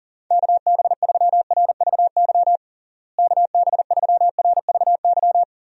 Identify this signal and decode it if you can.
Morse code, heard as a keyed tone, about 40 wpm, 700 Hz. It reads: KB3RVY KB3RVY
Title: USA Call Signs - 6 Characters - Repeated 3x - 40wpm